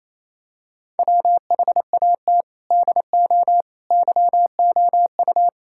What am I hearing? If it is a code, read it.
Morse code, heard as a keyed tone, about 28 wpm, 700 Hz. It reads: WHAT DO YOU
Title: Sentences from Top 1000 Words - Review Part 6 - 28wpm